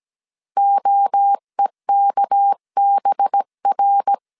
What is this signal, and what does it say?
Morse code, heard as a keyed tone, about 17 wpm, 800 Hz. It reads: OEKBR
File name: CW / DET / 1 6 1500C S C 16wpm 2733